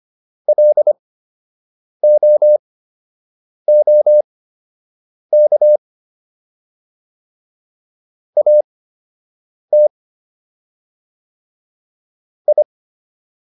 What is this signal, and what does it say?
Morse code, heard as a keyed tone, about 25 wpm, spaced as for 7 wpm, 600 Hz. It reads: LOOK AT I